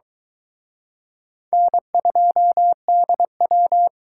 Morse code, heard as a keyed tone, about 22 wpm, 700 Hz. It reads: N2DW